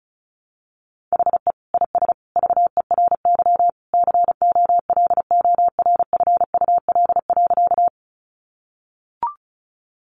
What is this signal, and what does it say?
Morse code, heard as a keyed tone, about 34 wpm, 700 Hz. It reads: HE IS VERY COLORFUL.
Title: Easy Phrases 35wpm